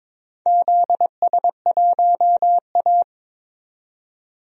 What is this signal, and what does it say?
Morse code, heard as a keyed tone, about 22 wpm, 700 Hz. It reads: ZS1A
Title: DXCC-100 Call Signs - 4 Characters - 22wpm